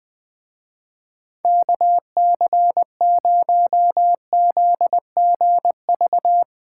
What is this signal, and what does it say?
Morse code, heard as a keyed tone, about 20 wpm, 700 Hz. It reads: KC0ZGV
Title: Speed Racing - USA 2x3 Call Signs 17wpm